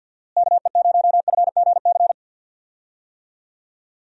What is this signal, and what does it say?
Morse code, heard as a keyed tone, about 50 wpm, 700 Hz. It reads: KE0FGC
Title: USA Call Signs - 6 Characters - Repeated 3x - 50wpm